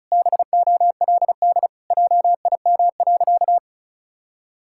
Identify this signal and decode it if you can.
Morse code, heard as a keyed tone, about 36 wpm, 700 Hz. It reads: BOLD JIM.
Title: Pangram Challenge v2 - 35wpm